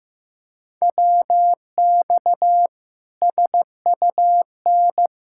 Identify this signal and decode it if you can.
Morse code, heard as a keyed tone, about 15 wpm, 700 Hz. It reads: WX SUN
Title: Practice QSO - Exchange 2 - Info - 15wpm